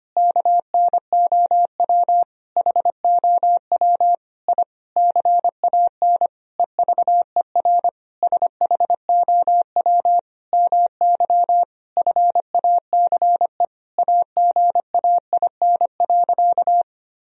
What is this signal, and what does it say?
Morse code, heard as a keyed tone, about 25 wpm, 700 Hz. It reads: KNOW HOW I CAN EVER SHOW MY FACE AGAIN.
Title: Sentences from Top 1000 Words - Review Part 2 - 25wpm